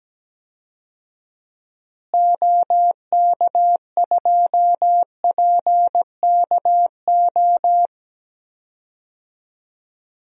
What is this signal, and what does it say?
Morse code, heard as a keyed tone, about 17 wpm, 700 Hz. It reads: OK2PKO
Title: DXCC-100 Call Signs - 6 Characters - Repeated 2x - 17wpm